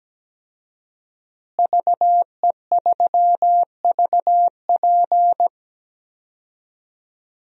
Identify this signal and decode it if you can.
Morse code, heard as a keyed tone, about 17 wpm, 700 Hz. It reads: VE3VP
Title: DXCC-100 Call Signs - 5 Characters - Repeated 2x - 17wpm